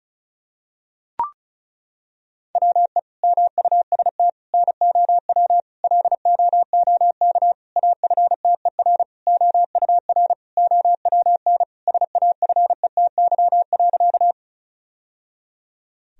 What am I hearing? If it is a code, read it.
Morse code, heard as a keyed tone, about 36 wpm, 700 Hz. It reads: WE MUST NOW LOOK AFTER OUR OWN SAFETY.
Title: Sentences from Top 1000 Words - Review Part 4 - 35wpm